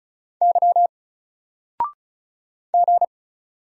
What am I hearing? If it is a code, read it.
Morse code, heard as a keyed tone, about 34 wpm, 700 Hz. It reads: Y G
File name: Single-Letter-Number-35wpm